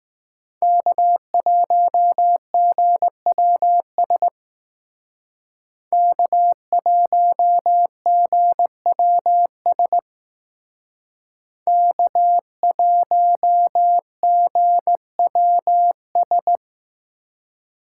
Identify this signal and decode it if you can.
Morse code, heard as a keyed tone, about 19 wpm, 700 Hz. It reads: K1GWS K1GWS K1GWS